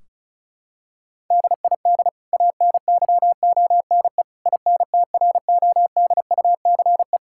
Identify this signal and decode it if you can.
Morse code, heard as a keyed tone, about 36 wpm, 700 Hz. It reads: DID ANYONE INTRODUCE
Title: Sentences from Top 1600 Words 35wpm